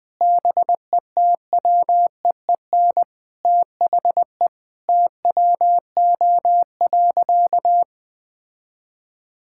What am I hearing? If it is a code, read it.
Morse code, heard as a keyed tone, about 21 wpm, 700 Hz. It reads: BETWEEN THE TWO.
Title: Sentences with 2-Word Phrases - 20wpm